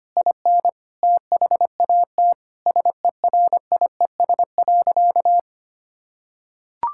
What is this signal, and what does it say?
Morse code, heard as a keyed tone, about 25 wpm, 700 Hz. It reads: IN THAT SERIES.
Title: Sentences from Top 1000 Words - Review Part 5 - 25wpm